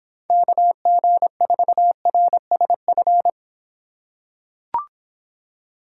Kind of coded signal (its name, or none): Morse code